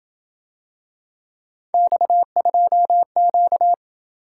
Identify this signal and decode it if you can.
Morse code, heard as a keyed tone, about 26 wpm, 700 Hz. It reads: X2Q